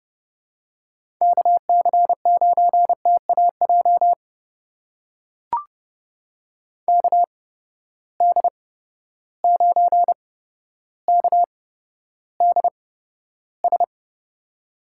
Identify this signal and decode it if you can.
Morse code, heard as a keyed tone, about 31 wpm, spaced as for 13 wpm, 700 Hz. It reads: KC9TAJ KD9KDS
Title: USA Call Signs - 6 Characters - 8x Character Spacing - 30wpm